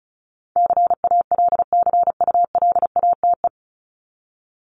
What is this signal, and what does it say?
Morse code, heard as a keyed tone, about 34 wpm, 700 Hz. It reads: CALCULATE
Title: Top 1751-2000 Words 35wpm